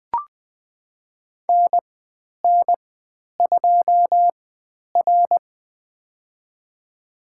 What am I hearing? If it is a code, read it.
Morse code, heard as a keyed tone, about 20 wpm, spaced as for 10 wpm, 700 Hz. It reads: NN2R
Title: 102 Lesson 17g - Letter u - Call Signs 20wpm